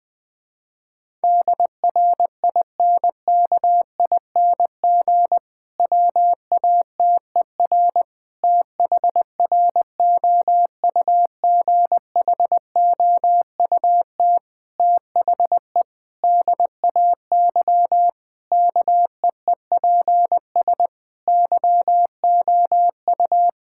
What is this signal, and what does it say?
Morse code, heard as a keyed tone, about 20 wpm, 700 Hz. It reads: DRINKING WATER THROUGHOUT THE DAY KEEPS YOU
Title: Difficult Sentences with Hints - 20wpm